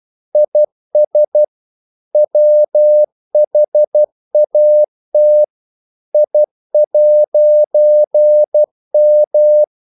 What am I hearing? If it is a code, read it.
Morse code, heard as a keyed tone, about 12 wpm, 600 Hz. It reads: IS WHAT I'M
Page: English quotations in CW (telegraphy - morse)